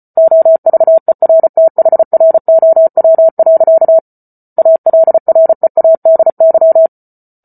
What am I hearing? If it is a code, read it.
Morse code, heard as a keyed tone, about 34 wpm, 650 Hz. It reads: OVERTHROW. ALREADY
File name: Book Two / 08-Chapter 8 - 34 WPM